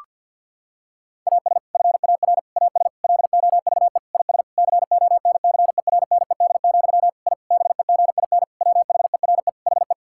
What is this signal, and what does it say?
Morse code, heard as a keyed tone, about 52 wpm, 700 Hz. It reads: AS FAR AS LOVE IS CONCERNED, I BEGIN WHERE HE
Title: Sentences from Top 1000 Words - Review Part 6 - 50wpm